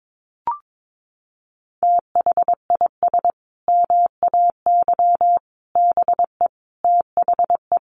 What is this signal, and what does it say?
Morse code, heard as a keyed tone, about 22 wpm, 700 Hz. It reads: THIS MAY BE THE
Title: Sentences from Top 900 Words 22wpm